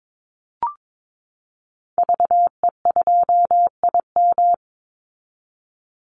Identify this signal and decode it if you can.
Morse code, heard as a keyed tone, about 22 wpm, 700 Hz. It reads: VE2IM
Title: Call Signs Repeated 2x - 22wpm